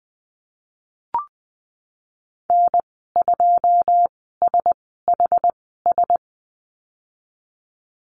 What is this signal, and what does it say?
Morse code, heard as a keyed tone, about 20 wpm, spaced as for 15 wpm, 700 Hz. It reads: N2SHS
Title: 082 Lesson 14g - Number 2 - Call Signs 20wpm